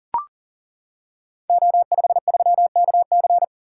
Morse code, heard as a keyed tone, about 38 wpm, 690 Hz. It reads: O53KC
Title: Code Groups - 5 Letter:Numbers - 40wpm